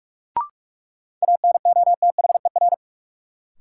Morse code, heard as a keyed tone, about 46 wpm, 700 Hz. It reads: ANOTHER
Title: Sentences from Top 600 Words - New Word Drill - 45wpm